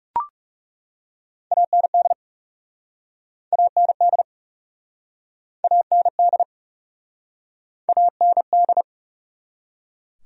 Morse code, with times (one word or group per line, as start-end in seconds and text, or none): 1.5-2.1 AND
3.5-4.2 AND
5.6-6.4 AND
7.9-8.8 AND